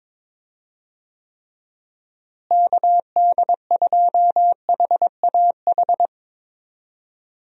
Morse code, heard as a keyed tone, about 22 wpm, 700 Hz. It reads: KD2HAH
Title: USA Call Signs - 6 Characters - Repeated 3x - 22wpm